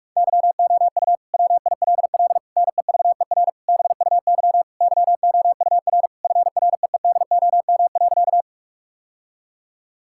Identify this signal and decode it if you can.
Morse code, heard as a keyed tone, about 46 wpm, 700 Hz. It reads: YOU WILL NEVER BUY YOUR FREEDOM.
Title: Sentences from Top 1000 Words - Review Part 5 - 45wpm